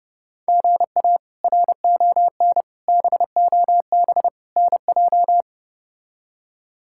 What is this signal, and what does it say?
Morse code, heard as a keyed tone, about 30 wpm, 700 Hz. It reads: GA RON BOB NJ
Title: K1USN SST QSO Practice - 30wpm